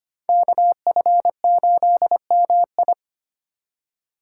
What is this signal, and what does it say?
Morse code, heard as a keyed tone, about 29 wpm, 700 Hz. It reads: KF8MI
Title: Speed Racing - USA 2x2 Call Signs 25wpm